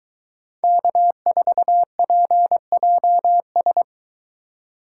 Morse code, heard as a keyed tone, about 20 wpm, 700 Hz. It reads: K4PJS